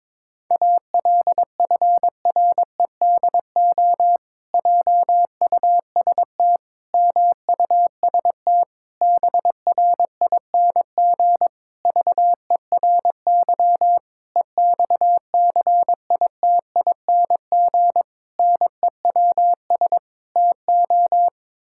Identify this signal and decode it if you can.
Morse code, heard as a keyed tone, about 22 wpm, 700 Hz. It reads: ALFREDO JUST MUST BRING VERY EXCITING NEWS TO
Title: Pangram Challenge v2 - 22wpm